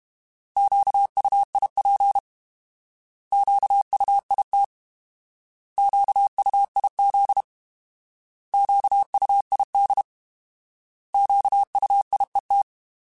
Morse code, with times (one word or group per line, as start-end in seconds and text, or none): 0.6-2.2 QUIP
3.3-4.6 QUIT
5.8-7.4 QUIZ
8.5-10.0 QUID
11.1-12.6 QUIET